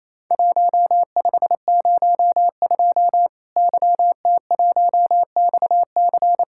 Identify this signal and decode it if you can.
Morse code, heard as a keyed tone, about 28 wpm, 700 Hz. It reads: 1502 YT1XC
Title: Worked All Europe DX Contest - QTC Exchange - 28wpm